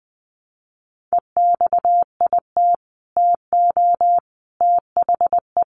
Morse code, heard as a keyed tone, about 20 wpm, 700 Hz. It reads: EXIT TO THE
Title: Sentences from Top 1100 Words 20wpm